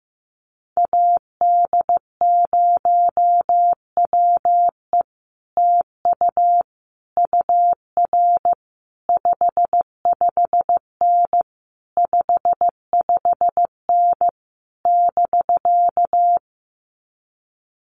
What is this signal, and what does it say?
Morse code, heard as a keyed tone, about 15 wpm, 700 Hz. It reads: AD0WE TU UR 55N 55N <BK>
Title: Common QSO Phrases Encore 15wpm